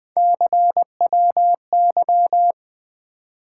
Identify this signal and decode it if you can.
Morse code, heard as a keyed tone, about 21 wpm, 700 Hz. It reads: CWY
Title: Three Letter-Number Combinations - 20wpm